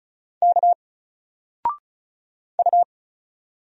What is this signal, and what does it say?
Morse code, heard as a keyed tone, about 35 wpm, 700 Hz. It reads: K U
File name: Single-Letter-Number-35wpm